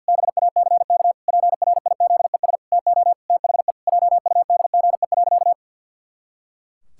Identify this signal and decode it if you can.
Morse code, heard as a keyed tone, about 52 wpm, 700 Hz. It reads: BACK PRIZES TO THE JUDGE.
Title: Pangram Challenge v2 - 50wpm